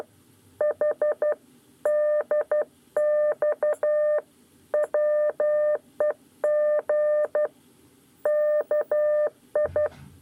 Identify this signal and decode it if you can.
Morse code, heard as a keyed tone, about 12 wpm, 600 Hz. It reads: H DX WEG KI
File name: 9 lezione cw 2